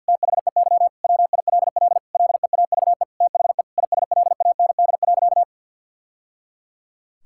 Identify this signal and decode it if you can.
Morse code, heard as a keyed tone, about 50 wpm, 700 Hz. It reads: THEY WILL LEAVE THE ISLAND.
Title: Sentences from Top 1000 Words - Review Part 6 - 50wpm